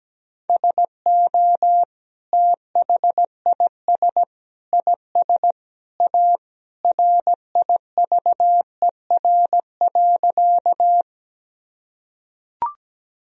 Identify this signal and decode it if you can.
Morse code, heard as a keyed tone, about 17 wpm, 700 Hz. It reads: SO THIS IS A RIVER.
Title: Sentences from Top 1000 Words - Review Part 1 - 17wpm